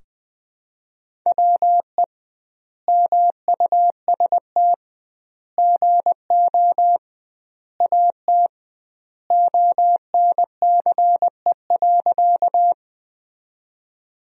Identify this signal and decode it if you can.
Morse code, heard as a keyed tone, about 20 wpm, 700 Hz. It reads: WE MUST GO AT ONCE.
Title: 2x Word-Spacing - Sentences from Top 300 Words - 20wpm